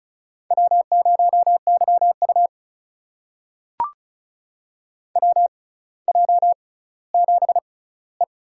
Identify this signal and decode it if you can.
Morse code, heard as a keyed tone, about 36 wpm, spaced as for 18 wpm, 700 Hz. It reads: W0YU WJ7E
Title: USA Call Signs - 4 Characters - 6x Character Spacing - 35wpm